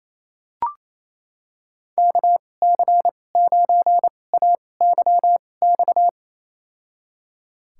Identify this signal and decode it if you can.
Morse code, heard as a keyed tone, about 28 wpm, spaced as for 20 wpm, 700 Hz. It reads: KC9AYX
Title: USA Call Signs - 6 Characters - 2x Character Spacing - 28wpm